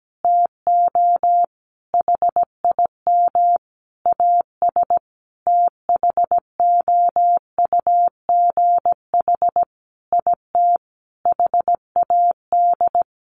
Morse code, heard as a keyed tone, about 17 wpm, 700 Hz. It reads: TO HIM AS THOUGH IT HAD
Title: Sentences from Top 1500 Words 17wpm